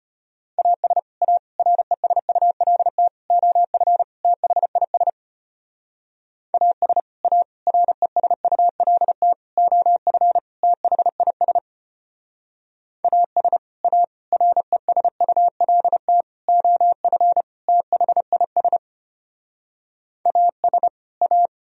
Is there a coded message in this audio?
Yes: Morse code from 0.6 to 21.5 s, heard as a keyed tone, about 32 wpm, 700 Hz. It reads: AS A RESULT OF THIS AS A RESULT OF THIS AS A RESULT OF THIS AS A